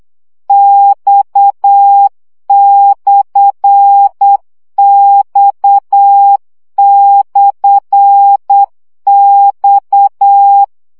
Morse code, heard as a keyed tone, about 8 wpm, 800 Hz. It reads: X/X/X
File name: CW gyakorlas x per 7 baud